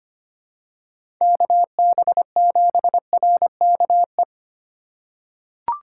Morse code, heard as a keyed tone, about 25 wpm, 700 Hz. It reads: KB7RKE
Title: USA Call Signs - 6 Characters - 4x Character Spacing - 25wpm